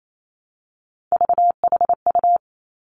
Morse code, heard as a keyed tone, about 28 wpm, 700 Hz. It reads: VHU
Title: Three Challenging Letter-Number Combinations 28wpm